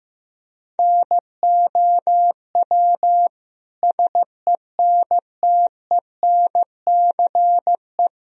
Morse code, heard as a keyed tone, about 15 wpm, 700 Hz. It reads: NOW SENTENCE
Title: Sentences from Top 1000 Words - Review Part 1 - 15wpm